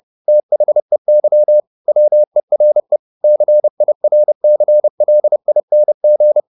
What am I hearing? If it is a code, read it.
Morse code, heard as a keyed tone, about 30 wpm, 600 Hz. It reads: THEY WERE CIRCLING
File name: the old man and the sea / the old man and the sea 1 30wpm